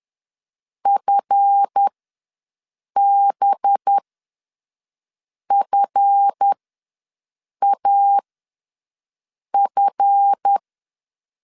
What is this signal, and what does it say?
Morse code, heard as a keyed tone, about 10 wpm, spaced as for 5 wpm, 800 Hz. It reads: FBFAF
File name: LEZIONI / 7 GF9Z / 7 GF ws 10wpm 3918